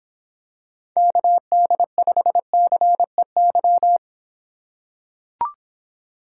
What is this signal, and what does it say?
Morse code, heard as a keyed tone, about 26 wpm, 700 Hz. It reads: KD5CEY